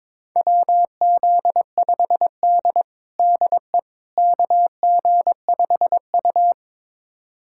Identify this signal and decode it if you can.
Morse code, heard as a keyed tone, about 22 wpm, 700 Hz. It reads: WZ5D DE KG5U
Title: USA Callsign DE Callsign - 4 Characters - 22wpm